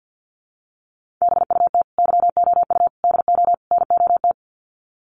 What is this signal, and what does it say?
Morse code, heard as a keyed tone, about 48 wpm, 700 Hz. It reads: BUT YOU DO NOT
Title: Sets of 4 Words 50wpm